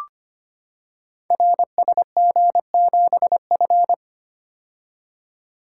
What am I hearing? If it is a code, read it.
Morse code, heard as a keyed tone, about 25 wpm, 700 Hz. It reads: RSG7F
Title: Code Groups - 5 Letter:Numbers - 25wpm